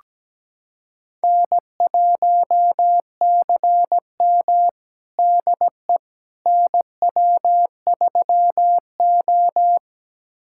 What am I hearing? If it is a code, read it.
Morse code, heard as a keyed tone, about 17 wpm, 700 Hz. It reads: N1CM DE NW3O
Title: USA Callsign DE Callsign - 4 Characters - 17wpm